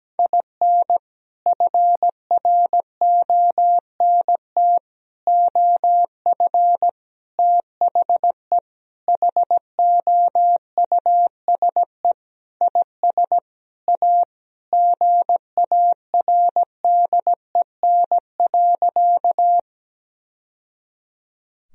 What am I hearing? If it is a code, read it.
Morse code, heard as a keyed tone, about 17 wpm, 700 Hz. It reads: IN FRONT OF THE HOUSE IS A GARDEN.
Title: Sentences from Top 1000 Words - Review Part 4 - 17wpm